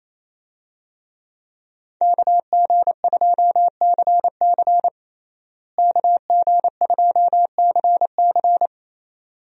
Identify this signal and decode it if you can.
Morse code, heard as a keyed tone, about 27 wpm, 700 Hz. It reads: KG2CC KG2CC
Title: USA Call Signs - 5 Characters - Repeated 2x - 28wpm